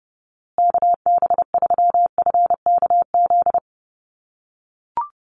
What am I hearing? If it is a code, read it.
Morse code, heard as a keyed tone, about 30 wpm, 700 Hz. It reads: KB3FKZ
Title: USA Call Signs Repeated 3x 30wpm